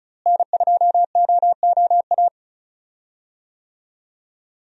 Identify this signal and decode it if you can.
Morse code, heard as a keyed tone, about 36 wpm, 700 Hz. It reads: N2OOA
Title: USA Call Signs - 5 Characters - Repeated 2x - 35wpm